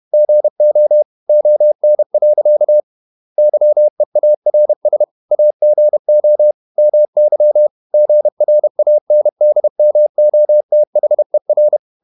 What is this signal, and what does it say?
Morse code, heard as a keyed tone, about 31 wpm, 600 Hz. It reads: GO ON. YEARS AGO MY GRANDMOTHER
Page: Never 31wpm | CW med Gnister